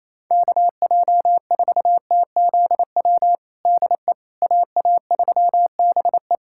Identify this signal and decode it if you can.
Morse code, heard as a keyed tone, about 28 wpm, 700 Hz. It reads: KJ4TZW DE AA3BE
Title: USA Callsign DE Callsign 28wpm